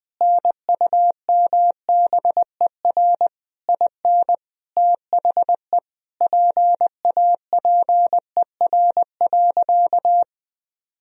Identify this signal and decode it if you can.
Morse code, heard as a keyed tone, about 19 wpm, 700 Hz. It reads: NUMBER IN THE PAPER.
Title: Sentences from Top 1000 Words - Review Part 1 - 20wpm